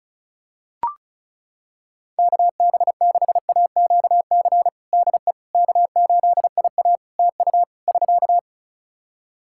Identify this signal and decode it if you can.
Morse code, heard as a keyed tone, about 36 wpm, 700 Hz. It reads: KB6AQC DE K8IA TU <SK>
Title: K1USN SST QSO Practice 35wpm